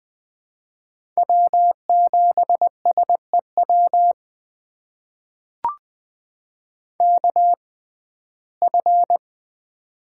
Morse code, heard as a keyed tone, about 20 wpm, spaced as for 15 wpm, 700 Hz. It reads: W7SEW KF
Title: USA Call Signs - 5 Characters - 6x Character Spacing - 20wpm